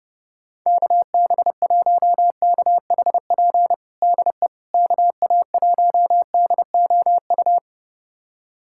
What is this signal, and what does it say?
Morse code, heard as a keyed tone, about 30 wpm, 700 Hz. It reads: KB1KHP DE KA1DOU
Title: USA Callsign DE Callsign 30wpm